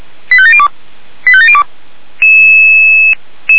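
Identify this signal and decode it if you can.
Five-tone selective call (ZVEI), heard as quick runs of five tones, tones of about 80 ms. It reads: 85702 85702
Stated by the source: ZVEI five-tone selective call 85827, received by SDR.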